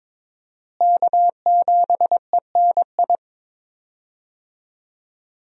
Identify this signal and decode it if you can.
Morse code, heard as a keyed tone, about 22 wpm, 700 Hz. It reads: K7ENI
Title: USA Call Signs - 5 Characters - Repeated 2x - 22wpm